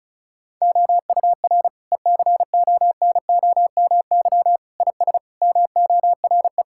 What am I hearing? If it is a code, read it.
Morse code, heard as a keyed tone, about 34 wpm, 700 Hz. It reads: OUR ECONOMY IS MORE
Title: Sentences from Top 1000 Words - Review Part 6 - 35wpm